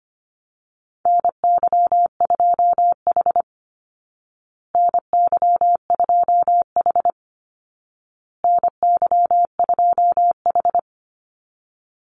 Morse code, heard as a keyed tone, about 25 wpm, 700 Hz. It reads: NY2H NY2H NY2H
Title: Call Signs Repeated 2x - 25wpm